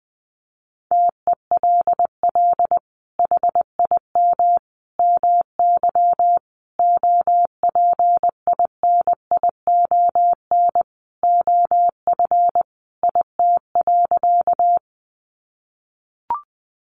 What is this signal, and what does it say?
Morse code, heard as a keyed tone, about 20 wpm, 700 Hz. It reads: TELL HIM MY OPINION OF IT.
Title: Sentences from Top 1200 Words 20wpm